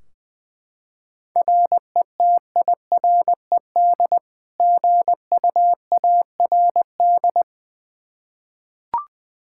Morse code, heard as a keyed tone, about 20 wpm, 700 Hz. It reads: RETIRED GUARD
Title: Practice QSO - Exchange 3 - More Info - 20wpm